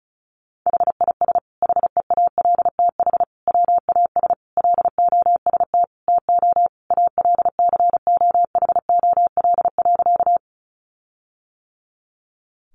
Morse code, heard as a keyed tone, about 34 wpm, 700 Hz. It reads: HIS HEALTH WAS LOST TO ALCOHOL.
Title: Sentences from Top 1400 Words 35wpm